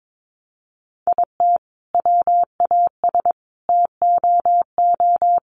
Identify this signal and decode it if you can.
Morse code, heard as a keyed tone, about 22 wpm, 700 Hz. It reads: IT WAS TOO
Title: Sentences from Top 1200 Words 22wpm